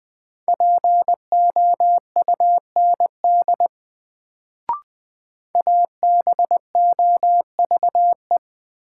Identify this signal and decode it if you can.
Morse code, heard as a keyed tone, about 21 wpm, 700 Hz. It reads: POUND ABOVE
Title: Sentences from Top 1000 Words - Review Part 2 - New Word Drill - 20wpm